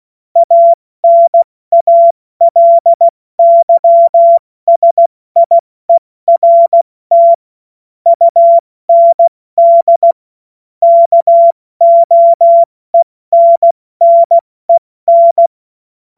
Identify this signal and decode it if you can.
Morse code, heard as a keyed tone, about 16 wpm, 680 Hz. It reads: ANALYSIERT UND KOENNEN